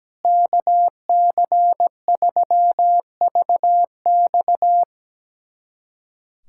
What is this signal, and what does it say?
Morse code, heard as a keyed tone, about 17 wpm, 700 Hz. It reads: KC3VX
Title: USA Callsign DE Callsign 17wpm